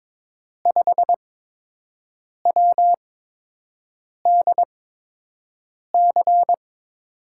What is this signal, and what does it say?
Morse code, heard as a keyed tone, about 21 wpm, spaced as for 6 wpm, 700 Hz. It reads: 5WDC
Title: USA Call Signs - 8x Character Spacing - 22wpm